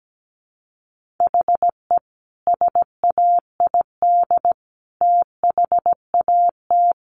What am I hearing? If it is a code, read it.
Morse code, heard as a keyed tone, about 17 wpm, 700 Hz. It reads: HE SAID THAT
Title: Sentences from Top 800 Words 17wpm